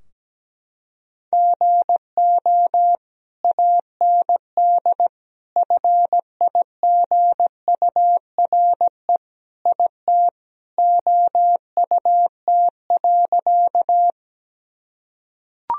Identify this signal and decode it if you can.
Morse code, heard as a keyed tone, about 17 wpm, 700 Hz. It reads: GO AND FIGURE IT OUT.
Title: Sentences from Top 1000 Words - Review Part 2 - 17wpm